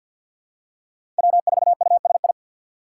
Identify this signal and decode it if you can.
Morse code, heard as a keyed tone, about 49 wpm, 700 Hz. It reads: W4USI
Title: K1USN SST QSO Practice - 50wpm